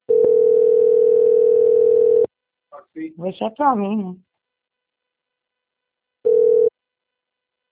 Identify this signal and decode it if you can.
Ringback tone (North America), 440 + 480 Hz, with speech.
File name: out- - - - - .36083